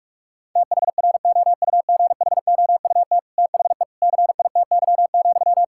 Morse code, heard as a keyed tone, about 46 wpm, 700 Hz. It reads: THROUGHOUT THE CITY,
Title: Sentences from Top 1000 Words - Review Part 5 - 45wpm